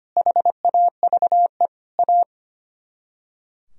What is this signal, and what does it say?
Morse code, heard as a keyed tone, about 31 wpm, 700 Hz. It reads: HAVE A
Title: Speed-Racing - Sets of 3 Words 25wpm